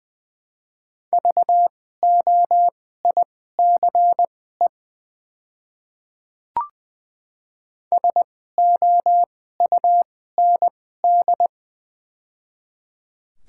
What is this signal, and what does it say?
Morse code, heard as a keyed tone, about 20 wpm, spaced as for 14 wpm, 700 Hz. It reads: VOICE SOUND
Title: 2x Character Spacing - Top 500 Words - 20wpm